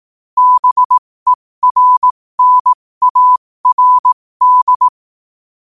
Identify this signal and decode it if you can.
Morse code, heard as a keyed tone, about 18 wpm, spaced as for 15 wpm, 1000 Hz. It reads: BERNARD
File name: qotdfiles / 15wpm